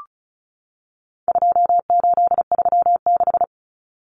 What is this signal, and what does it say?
Morse code, heard as a keyed tone, about 34 wpm, 700 Hz. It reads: 2836
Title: Common Cut Numbers - 4-Digit 35wpm